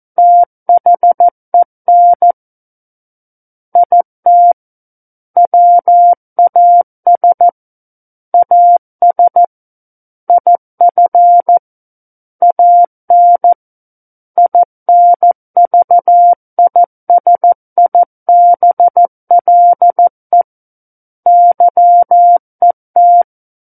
Morse code, heard as a keyed tone, about 14 wpm, 700 Hz. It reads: THEN IT WAS AS IF AN INVISIBLE YET
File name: Book One / 05-Chapter 5 - 14 WPM